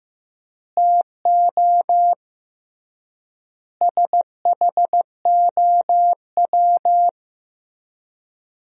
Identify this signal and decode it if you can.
Morse code, heard as a keyed tone, about 15 wpm, 700 Hz. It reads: TO SHOW
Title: 3x Word-Spacing - Sentences from Top 200 Words - 15wpm